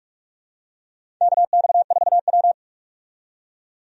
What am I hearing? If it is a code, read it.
Morse code, heard as a keyed tone, about 46 wpm, 700 Hz. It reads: KX4W
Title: USA Call Signs - 4 Characters - 45wpm